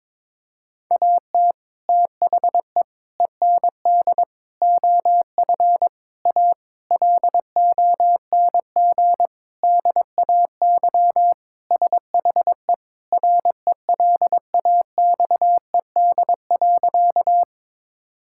Morse code, heard as a keyed tone, about 22 wpm, 700 Hz. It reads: AT THE END OF A LONG DAY SHE RELAXED.
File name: practice / 22wpm / Sentences with 4-Word Phrases - 22wpm